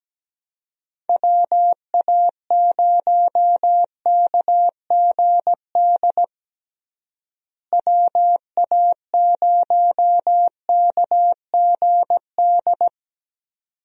Morse code, heard as a keyed tone, about 17 wpm, 700 Hz. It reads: WA0KGD WA0KGD
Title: USA Call Signs - 6 Characters - Repeated 2x - 17wpm